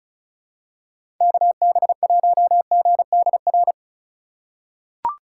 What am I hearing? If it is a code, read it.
Morse code, heard as a keyed tone, about 36 wpm, 700 Hz. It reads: KB1GDR
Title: USA Call Signs - 6 Characters - 3x Character Spacing - 35wpm